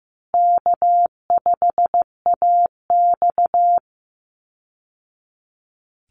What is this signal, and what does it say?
Morse code, heard as a keyed tone, about 15 wpm, 700 Hz. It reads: K5AX
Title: Easy Call Signs Repeated 2x - 15wpm